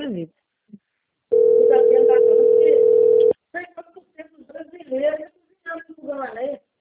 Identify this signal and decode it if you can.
Ringback tone (North America), 440 + 480 Hz, with speech.
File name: rg-1600-unknown- - - .24062